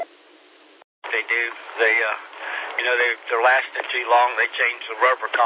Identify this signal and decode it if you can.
Emergency Alert System (SAME) tones, heard as harsh buzzing bursts, with speech. It is heard from 1.1 to 4.8 s.